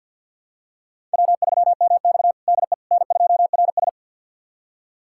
Morse code, heard as a keyed tone, about 52 wpm, 700 Hz. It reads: W3MX DE N2RS
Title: USA Callsign DE Callsign - 4 Characters - 50wpm